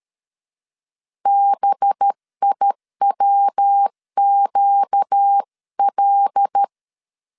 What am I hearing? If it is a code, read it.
Morse code, heard as a keyed tone, about 13 wpm, 800 Hz. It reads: BIWQL
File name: LEZIONI / 7 GF9Z / 7 GF 12wpm 2122